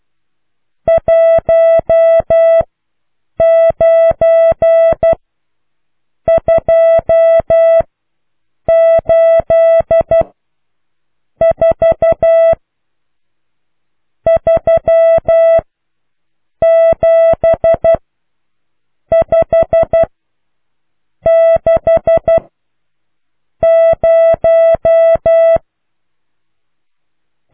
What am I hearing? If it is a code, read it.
Morse code, heard as a keyed tone, about 12 wpm, spaced as for 6 wpm, 660 Hz. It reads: 19284 37560